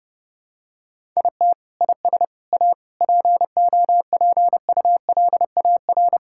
Morse code, heard as a keyed tone, about 32 wpm, 700 Hz. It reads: IT IS A POPULAR
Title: Sentences from Top 1000 Words - Review Part 5 - 30wpm